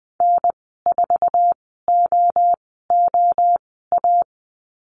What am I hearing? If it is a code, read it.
Morse code, heard as a keyed tone, about 20 wpm, spaced as for 15 wpm, 700 Hz. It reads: N4OOA
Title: 055 Lesson 10g - Letter r - Call Signs 20wpm